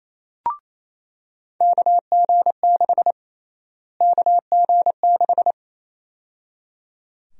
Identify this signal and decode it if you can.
Morse code, heard as a keyed tone, about 28 wpm, 700 Hz. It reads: KG6 KG6
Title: USA Call Sign Prefixes - Repeated 2x - 28wpm